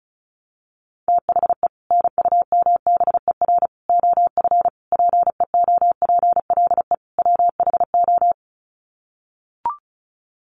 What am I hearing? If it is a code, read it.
Morse code, heard as a keyed tone, about 34 wpm, 700 Hz. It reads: THE NUMBER OF PEOPLE WHO
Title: Sets of 5 Words Encore 35wpm